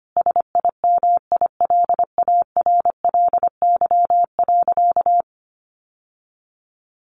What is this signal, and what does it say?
Morse code, heard as a keyed tone, about 25 wpm, 700 Hz. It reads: SIMILARLY.
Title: Sentences from Top 1400 Words 25wpm